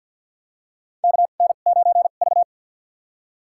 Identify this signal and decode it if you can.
Morse code, heard as a keyed tone, about 51 wpm, spaced as for 36 wpm, 700 Hz. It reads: KN9V
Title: USA Call Signs - 4 Characters - 2x Character Spacing - 50wpm